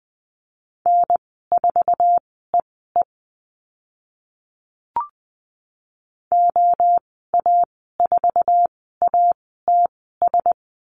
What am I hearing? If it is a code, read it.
Morse code, heard as a keyed tone, about 20 wpm, spaced as for 15 wpm, 700 Hz. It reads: N4EE OA4ATS
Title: 062 Lesson 11g - Letter h - Call Signs 20wpm